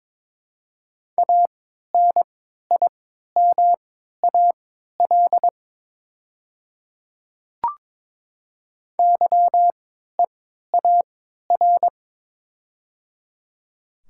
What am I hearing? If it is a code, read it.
Morse code, heard as a keyed tone, about 23 wpm, spaced as for 13 wpm, 700 Hz. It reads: ANIMAL YEAR
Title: 3x Character Spacing - Top 300 Words - 22wpm